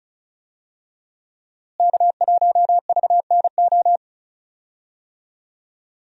Morse code, heard as a keyed tone, about 36 wpm, 700 Hz. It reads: K1VNO
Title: USA Call Signs - 5 Characters - Repeated 2x - 35wpm